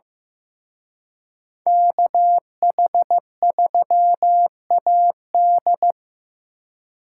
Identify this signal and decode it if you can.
Morse code, heard as a keyed tone, about 18 wpm, 700 Hz. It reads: KH3AD
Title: Speed Racing - USA 2x2 Call Signs 15wpm